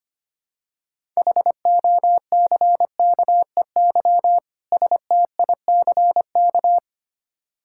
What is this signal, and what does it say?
Morse code, heard as a keyed tone, about 25 wpm, 700 Hz. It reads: HOCKEY STICK
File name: practice / 25wpm / Difficult Words with Hints - 25wpm